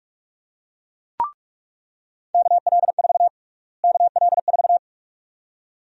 Morse code, heard as a keyed tone, about 46 wpm, 700 Hz. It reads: KL4 KL4
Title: USA Call Sign Prefixes - Repeated 2x - 45wpm